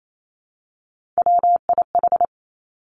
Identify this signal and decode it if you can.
Morse code, heard as a keyed tone, about 28 wpm, 700 Hz. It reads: WIH
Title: Three Challenging Letter-Number Combinations 28wpm